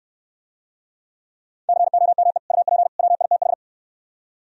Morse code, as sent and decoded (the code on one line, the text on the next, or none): -.-. --- -- . / ..- .--. / .-- .. - ....
COME UP WITH